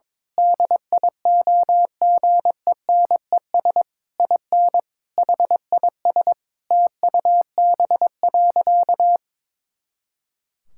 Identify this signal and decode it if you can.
Morse code, heard as a keyed tone, about 22 wpm, 700 Hz. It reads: DIOGENES IN HIS TUB.
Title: Pangram Challenge v2 - 22wpm